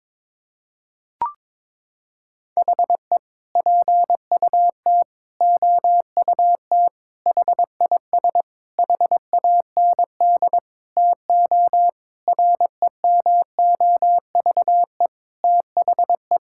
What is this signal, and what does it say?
Morse code, heard as a keyed tone, about 21 wpm, 700 Hz. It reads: HE PUT OUT HIS HAND TO REMOVE THE
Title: Sentences from Top 1000 Words - Review Part 5 - 22wpm